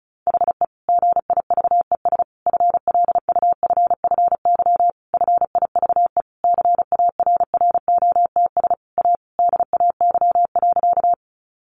Morse code, heard as a keyed tone, about 34 wpm, 700 Hz. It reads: HE GIVES FLUFFY FIVE CARROTS A DAY.
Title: Easy Phrases 35wpm